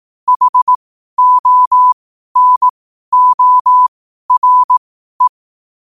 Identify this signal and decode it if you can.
Morse code, heard as a keyed tone, about 18 wpm, spaced as for 12 wpm, 1000 Hz. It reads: HONORE
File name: qotdfiles / 12wpm